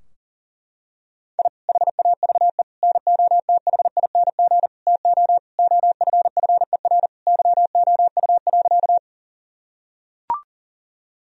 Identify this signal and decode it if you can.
Morse code, heard as a keyed tone, about 40 wpm, 700 Hz. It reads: I HAVE NOTHING TO OFFER YOU.
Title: Sentences from Top 1000 Words - Review Part 4 - 40wpm